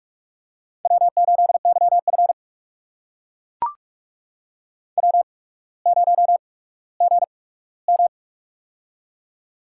Morse code, heard as a keyed tone, about 44 wpm, spaced as for 19 wpm, 700 Hz. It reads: W8YF W0GM
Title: USA Call Signs - 4 Characters - 8x Character Spacing - 45wpm